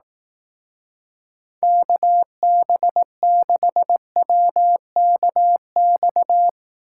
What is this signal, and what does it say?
Morse code, heard as a keyed tone, about 18 wpm, 700 Hz. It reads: KB6WKX
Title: Speed Racing - USA 2x3 Call Signs 15wpm